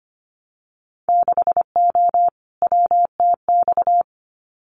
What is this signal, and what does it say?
Morse code, heard as a keyed tone, about 25 wpm, 700 Hz. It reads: 6O WTX
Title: Winter Field Day Exchanges Encore 25wpm